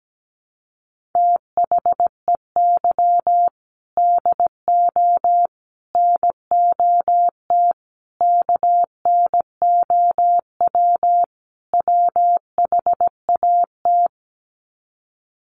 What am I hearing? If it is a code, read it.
Morse code, heard as a keyed tone, about 17 wpm, 700 Hz. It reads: THEY DO NOT KNOW WHAT
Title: Sets of 5 Words Encore 17wpm